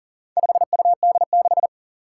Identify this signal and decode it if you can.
Morse code, heard as a keyed tone, about 43 wpm, 700 Hz. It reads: 5UD6
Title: Four Challenging Letter-Number Combinations 40wpm